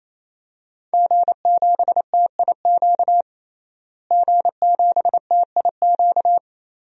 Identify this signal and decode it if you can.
Morse code, heard as a keyed tone, about 28 wpm, 700 Hz. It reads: G7TIQ G7TIQ
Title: DXCC-100 Call Signs - 5 Characters - Repeated 3x - 28wpm